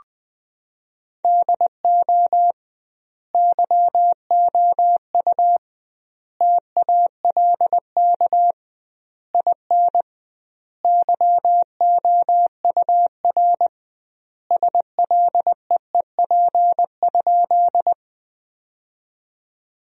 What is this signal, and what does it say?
Morse code, heard as a keyed tone, about 19 wpm, 700 Hz. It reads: DO YOU TALK IN YOUR SLEEP?
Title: 2x Word-Spacing - Sentences from Top 500 Words - 20wpm